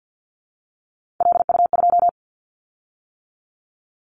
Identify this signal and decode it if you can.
Morse code, heard as a keyed tone, about 50 wpm, 700 Hz. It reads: LU2
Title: Callsign Prefixes for Top 50 DXCC - 50wpm